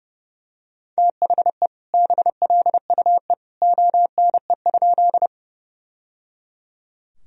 Morse code, heard as a keyed tone, about 30 wpm, 700 Hz. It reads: THE BLUE ONE?
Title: Sentences from Top 1000 Words - Review Part 3 - 30wpm